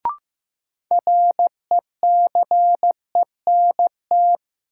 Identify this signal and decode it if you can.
Morse code, heard as a keyed tone, about 15 wpm, 700 Hz. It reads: RECENT
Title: Sentences from Top 1400 Words - New Word Drill Alt - 15wpm